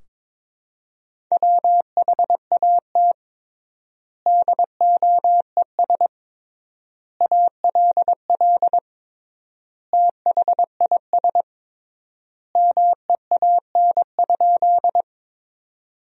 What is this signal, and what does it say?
Morse code, heard as a keyed tone, about 22 wpm, 700 Hz. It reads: WHAT DOES ALL THIS MEAN?
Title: 3x Word-Spacing - Sentences from Top 200 Words - 22wpm